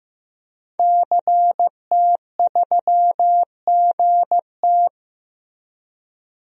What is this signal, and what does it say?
Morse code, heard as a keyed tone, about 16 wpm, 700 Hz. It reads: CT3GT
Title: DXCC-100 Call Signs - 5 Characters - Repeated 2x - 15wpm